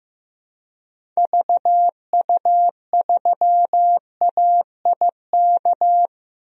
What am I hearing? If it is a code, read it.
Morse code, heard as a keyed tone, about 15 wpm, 700 Hz. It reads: VU3AIK
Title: DXCC-100 Call Signs - 6 Characters - 15wpm